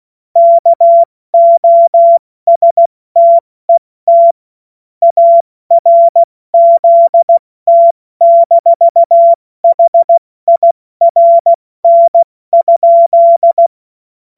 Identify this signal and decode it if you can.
Morse code, heard as a keyed tone, about 16 wpm, 680 Hz. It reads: KOSTET ARZT-HIRN?
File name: Morsekurs / Witze / Witz 07. Juni 2022 70 F80